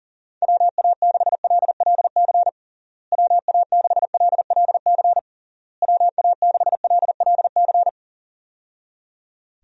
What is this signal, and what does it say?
Morse code, heard as a keyed tone, about 40 wpm, 700 Hz. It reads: WA6LLC WA6LLC WA6LLC
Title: USA Call Signs - 6 Characters - Repeated 3x - 40wpm